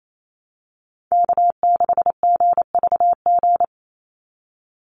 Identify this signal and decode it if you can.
Morse code, heard as a keyed tone, about 28 wpm, 700 Hz. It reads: K6GVG
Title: Call Signs Repeated 3x - 28wpm